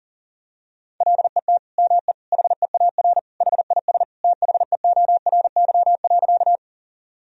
Morse code, heard as a keyed tone, about 40 wpm, 700 Hz. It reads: LET ME HEAR HIS THEORY.
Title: Sentences from Top 1000 Words - Review Part 3 - 40wpm